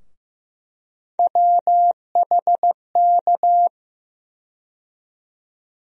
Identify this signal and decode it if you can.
Morse code, heard as a keyed tone, about 15 wpm, 700 Hz. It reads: WHK
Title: USA Call Sign Postfixes - 3 Letters 15wpm